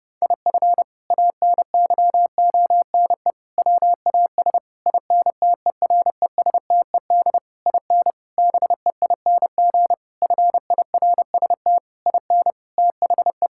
Morse code, heard as a keyed tone, about 30 wpm, 700 Hz. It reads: IF ANYONE WAS INTERESTED IN BEING FIRST IN THE
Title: Sentences from Top 1600 Words 30wpm